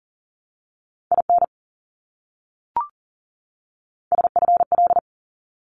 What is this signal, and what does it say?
Morse code, heard as a keyed tone, about 40 wpm, 700 Hz. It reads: IN SFL
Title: ARRL Sections 40wpm